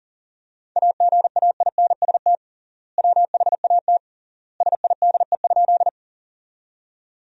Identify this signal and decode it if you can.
Morse code, heard as a keyed tone, about 40 wpm, 700 Hz. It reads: AGAINST WHAT SIDE?
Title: 3x Word-Spacing - Sentences from Top 400 Words - 40wpm